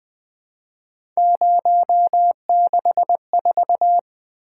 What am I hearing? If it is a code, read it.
Morse code, heard as a keyed tone, about 20 wpm, 700 Hz. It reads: 064
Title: Three Letter-Number Combinations - 20wpm